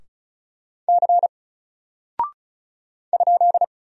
Morse code, heard as a keyed tone, about 36 wpm, 700 Hz. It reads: C ?
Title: Single Character (Letter:Number:Punctuation:Prosign) 35wpm